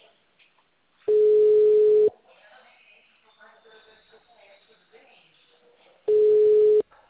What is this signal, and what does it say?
Ringback tone (Europe, 425 Hz), with speech.
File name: out- -2074- - - .17